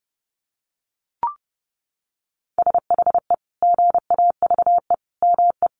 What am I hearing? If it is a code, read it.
Morse code, heard as a keyed tone, about 30 wpm, 700 Hz. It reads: SHE GAVE ME
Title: Sentences from Top 1400 Words 30wpm